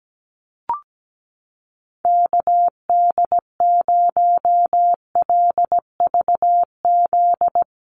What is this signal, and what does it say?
Morse code, heard as a keyed tone, about 17 wpm, 700 Hz. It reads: KD0LVZ
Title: USA Call Signs Repeated 3x 17wpm